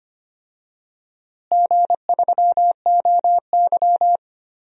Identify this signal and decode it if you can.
Morse code, heard as a keyed tone, about 25 wpm, 700 Hz. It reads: G3OY
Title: DXCC-100 Call Signs - 4 Characters - Repeated 3x - 25wpm